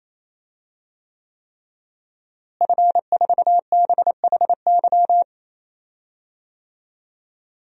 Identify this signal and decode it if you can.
Morse code, heard as a keyed tone, about 28 wpm, 700 Hz. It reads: F4BHY
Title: DXCC-100 Call Signs - 5 Characters - Repeated 3x - 28wpm